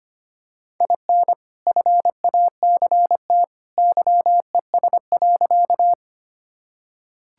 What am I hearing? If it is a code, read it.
Morse code, heard as a keyed tone, about 25 wpm, 700 Hz. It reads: IN FACT YES.
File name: practice / 25wpm / Sentences with 2-Word Phrases - 25wpm